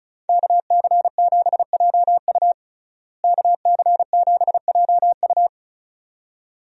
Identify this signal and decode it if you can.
Morse code, heard as a keyed tone, about 36 wpm, 700 Hz. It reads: KC7JU KC7JU
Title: USA Call Signs - 5 Characters - Repeated 2x - 35wpm